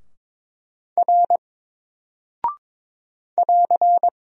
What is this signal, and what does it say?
Morse code, heard as a keyed tone, about 22 wpm, 700 Hz. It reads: R <AR>
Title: Single Character (Letter:Number:Punctuation:Prosign) Alt 22wpm